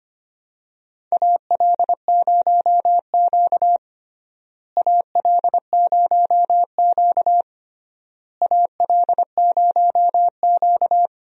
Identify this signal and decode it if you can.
Morse code, heard as a keyed tone, about 25 wpm, 700 Hz. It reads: AL0Q AL0Q AL0Q
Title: USA Call Signs - 4 Characters - Repeated 3x - 25wpm